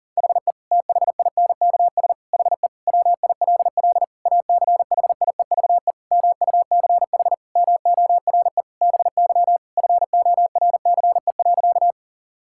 Morse code, heard as a keyed tone, about 40 wpm, 700 Hz. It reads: HE THINKS HE WILL ACHIEVE MUCH MORE BY FORCE.
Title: Sentences from Top 1000 Words - Review Part 6 - 40wpm